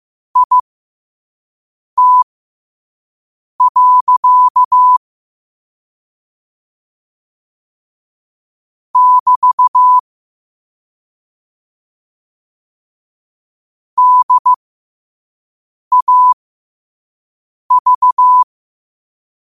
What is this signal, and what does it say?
Morse code, heard as a keyed tone, about 15 wpm, spaced as for 5 wpm, 1000 Hz. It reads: IT. <BT> DAV
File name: qotdfiles / 5wpm